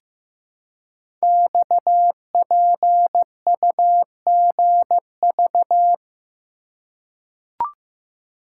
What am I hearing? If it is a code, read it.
Morse code, heard as a keyed tone, about 15 wpm, 700 Hz. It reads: XPUGV
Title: Code Groups - 5 Letters - 15wpm